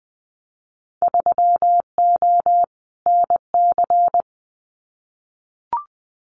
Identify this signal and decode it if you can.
Morse code, heard as a keyed tone, about 20 wpm, 700 Hz. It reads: 3O NC
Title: Winter Field Day Exchanges 20wpm